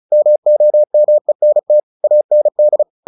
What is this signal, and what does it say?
Morse code, heard as a keyed tone, about 34 wpm, 600 Hz. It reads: MOMENT AND
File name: the old man and the sea 2 35wpm